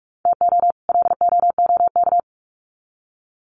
Morse code, heard as a keyed tone, about 44 wpm, 700 Hz. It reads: TO LOOK
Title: Sets of 2 Words 45wpm